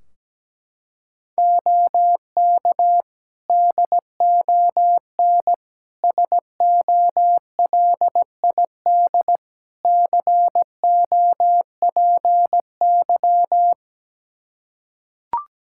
Morse code, heard as a keyed tone, about 17 wpm, 700 Hz. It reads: OK DON SOLID COPY
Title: Practice QSO - Exchange 3 - More Info - 17wpm